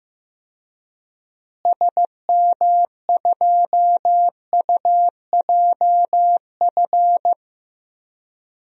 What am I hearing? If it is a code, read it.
Morse code, heard as a keyed tone, about 15 wpm, 700 Hz. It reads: SM2UJF
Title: DXCC-100 Call Signs - 6 Characters - Repeated 2x - 15wpm